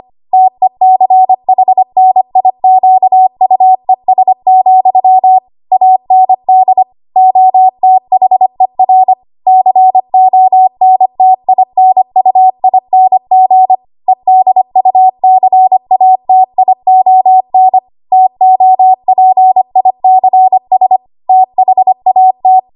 Morse code, heard as a keyed tone, about 25 wpm, 750 Hz. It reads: TECHNIQUES, AND OTHER CONTINUING EDUCATION TOPICS THAT
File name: Archive / 25 WPM / 25WPM